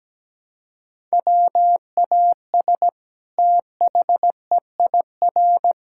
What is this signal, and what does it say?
Morse code, heard as a keyed tone, about 17 wpm, 700 Hz. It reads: WAS THEIR
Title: Sentences from Top 1000 Words - Review Part 6 - 17wpm